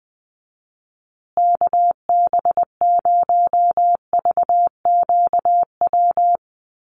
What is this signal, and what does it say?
Morse code, heard as a keyed tone, about 20 wpm, 700 Hz. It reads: KB0VQW
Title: USA Call Signs Repeated 2x 20wpm